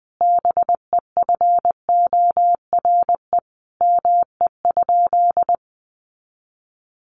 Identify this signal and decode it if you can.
Morse code, heard as a keyed tone, about 20 wpm, 700 Hz. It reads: BEFORE ME?
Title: Sentences from Top 200 Words 20wpm